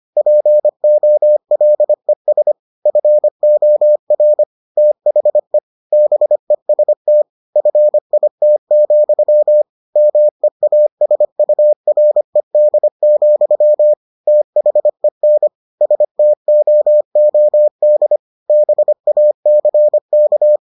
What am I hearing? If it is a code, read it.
Morse code, heard as a keyed tone, about 25 wpm, 600 Hz. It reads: POLES FOR THE BEST FIT, MEASURED, THEN STOOD BACK